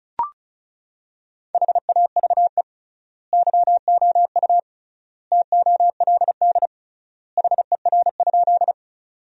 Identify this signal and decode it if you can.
Morse code, heard as a keyed tone, about 34 wpm, 700 Hz. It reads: HAVE YOU TOLD HER?
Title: 3x Word-Spacing - Sentences from Top 400 Words - 35wpm